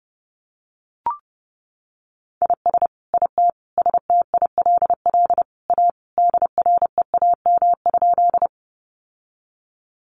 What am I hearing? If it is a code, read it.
Morse code, heard as a keyed tone, about 28 wpm, 700 Hz. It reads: IS IT STILL A DREAM?
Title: Sentences from Top 1500 Words 30wpm